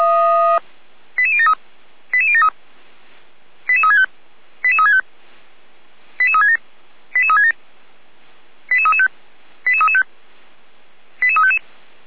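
Five-tone selective call (ZVEI), heard as quick runs of five tones, tones of about 70 ms. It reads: 88073 88073 88365 88365 88367 88367 88335 88335 88355
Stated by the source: ZVEI five-tone selective call 89110 received by SDR.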